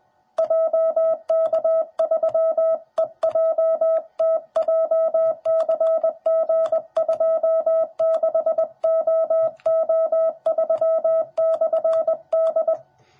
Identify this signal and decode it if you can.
Morse code, heard as a keyed tone, about 20 wpm, 650 Hz. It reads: JK3EJT JCG26OO3/D
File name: jh8cbh / 2021cw / cw10b